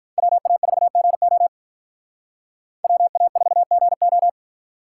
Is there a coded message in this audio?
Yes: Morse code from 0.2 to 4.3 s, heard as a keyed tone, about 45 wpm, 700 Hz. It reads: WA4GO WA4GO